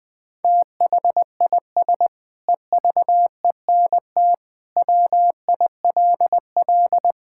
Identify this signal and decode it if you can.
Morse code, heard as a keyed tone, about 20 wpm, 700 Hz. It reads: THIS EVENT WILL
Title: Sentences with 9-Word Phrases - Spoken Beginning - 20wpm